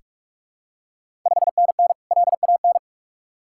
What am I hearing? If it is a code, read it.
Morse code, heard as a keyed tone, about 44 wpm, 690 Hz. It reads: 5NN LAN
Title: Ontario QSO Party Exchange 45wpm